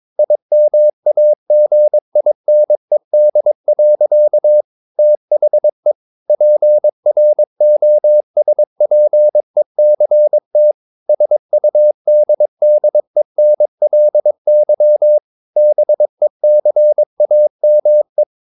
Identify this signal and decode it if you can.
Morse code, heard as a keyed tone, about 22 wpm, 600 Hz. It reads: IMAGINED. THE PROSPECT SUDDENLY BECAME
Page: Romanen Never af Ken Follet på engelsk.